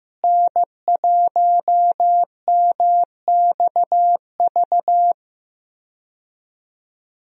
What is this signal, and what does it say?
Morse code, heard as a keyed tone, about 15 wpm, 700 Hz. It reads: N1MXV
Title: USA Call Signs - 5 Characters - Repeated 3x - 15wpm